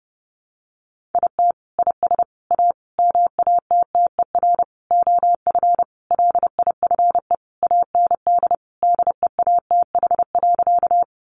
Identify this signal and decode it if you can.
Morse code, heard as a keyed tone, about 30 wpm, 700 Hz. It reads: IT IS A MATTER OF LIFE AND DEATH.
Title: Sentences from Top 1100 Words 30wpm